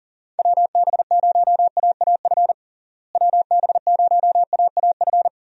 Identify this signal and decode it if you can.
Morse code, heard as a keyed tone, about 40 wpm, 700 Hz. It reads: WB0AAF WB0AAF
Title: USA Call Signs - 6 Characters - Repeated 3x - 40wpm